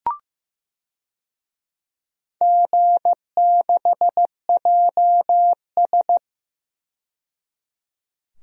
Morse code, heard as a keyed tone, about 15 wpm, 700 Hz. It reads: G6JS
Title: DXCC-100 Call Signs - 4 Characters - 15wpm